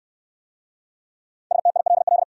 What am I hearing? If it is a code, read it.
Morse code, heard as a keyed tone, about 59 wpm, 700 Hz. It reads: STILL